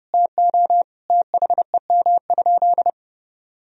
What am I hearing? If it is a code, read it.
Morse code, heard as a keyed tone, about 32 wpm, 700 Hz. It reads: TO THEM?
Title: Sentences from Top 1000 Words - Review Part 2 - 30wpm